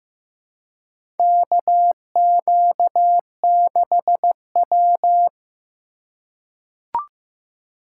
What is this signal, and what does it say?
Morse code, heard as a keyed tone, about 15 wpm, 700 Hz. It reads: KQ6W
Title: USA Call Signs - 2x Character Spacing - 15wpm